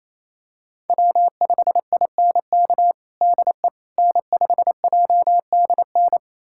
Morse code, heard as a keyed tone, about 28 wpm, 700 Hz. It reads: W5INK DE N5JDN
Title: USA Callsign DE Callsign - 5 Characters - 28wpm